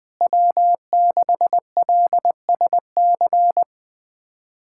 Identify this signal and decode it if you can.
Morse code, heard as a keyed tone, about 19 wpm, 700 Hz. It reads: W6LSC
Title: USA Callsign DE Callsign 20wpm